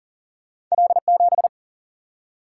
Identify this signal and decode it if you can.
Morse code, heard as a keyed tone, about 38 wpm, 700 Hz. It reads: L7
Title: Single-Letter Single-Number 40wpm